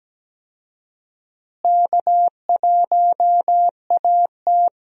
Morse code, heard as a keyed tone, about 17 wpm, 700 Hz. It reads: K1AT
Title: USA Call Signs - 4 Characters - Repeated 3x - 17wpm